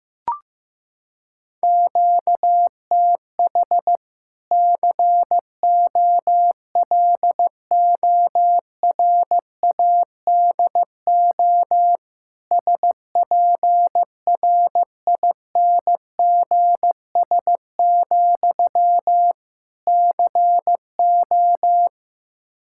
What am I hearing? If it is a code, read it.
Morse code, heard as a keyed tone, about 15 wpm, 700 Hz. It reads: QTH COLORADO SPRINGS, CO
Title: Practice QSO - Exchange 1 - Beginning - 15wpm